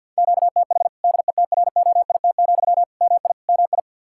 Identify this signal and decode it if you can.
Morse code, heard as a keyed tone, about 51 wpm, 700 Hz. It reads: QTH DETROIT, MI MI
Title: Practice QSO - Exchange 1 - Beginning - 50wpm